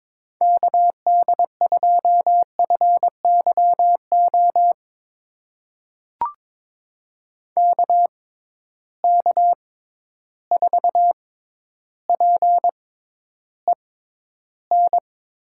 Morse code, heard as a keyed tone, about 22 wpm, spaced as for 11 wpm, 700 Hz. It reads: KD2FYO KK4PEN
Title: USA Call Signs - 6 Characters - 6x Character Spacing - 22wpm